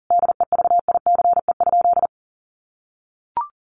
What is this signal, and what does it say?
Morse code, heard as a keyed tone, about 37 wpm, 700 Hz. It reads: DEVICE?
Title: Sentences from Top 1500 Words 40wpm